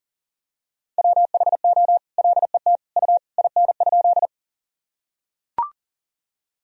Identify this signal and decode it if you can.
Morse code, heard as a keyed tone, about 40 wpm, 700 Hz. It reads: WHO LET U IN?
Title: Sentences from Top 300 Words and Abbreviations 40wpm